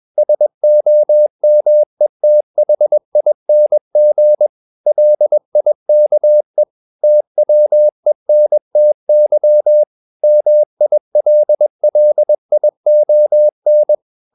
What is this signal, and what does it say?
Morse code, heard as a keyed tone, about 21 wpm, 600 Hz. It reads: SOMETHING LIKE TWENTY MILLION
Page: CW med Gnister